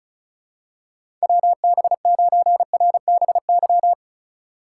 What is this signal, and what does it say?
Morse code, heard as a keyed tone, about 36 wpm, 700 Hz. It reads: WB9RBY
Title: USA Call Signs - 2x Character Spacing - 35wpm